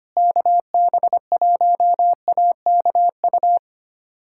Morse code, heard as a keyed tone, about 25 wpm, 700 Hz. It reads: KB1AKU
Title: USA Callsign DE Callsign 25wpm